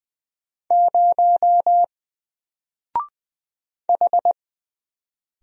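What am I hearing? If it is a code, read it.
Morse code, heard as a keyed tone, about 20 wpm, 700 Hz. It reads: 0 H
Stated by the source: Single letter and number Morse drill at 20 wpm, from N4LCD.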